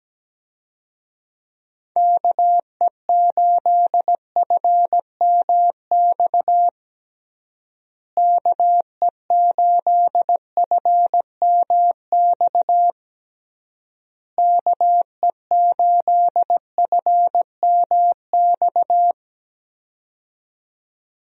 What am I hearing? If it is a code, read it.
Morse code, heard as a keyed tone, about 17 wpm, 700 Hz. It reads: KE8FMX KE8FMX KE8FMX
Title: USA Call Signs - 6 Characters - Repeated 3x - 17wpm